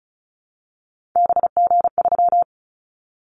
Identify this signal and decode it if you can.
Morse code, heard as a keyed tone, about 34 wpm, 700 Hz. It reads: BG3
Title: Callsign Prefixes for Top 100 DXCC - 35wpm